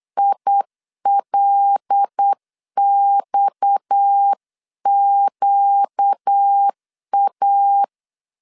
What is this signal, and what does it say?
Morse code, heard as a keyed tone, about 8 wpm, 800 Hz. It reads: ILXQA
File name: LEZIONI / 6 PKX0 / 6 PKX 8wpm 3041